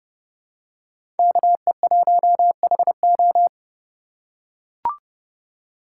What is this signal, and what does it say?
Morse code, heard as a keyed tone, about 30 wpm, 700 Hz. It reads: KE1HO
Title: USA Call Signs - 5 Characters - 4x Character Spacing - 30wpm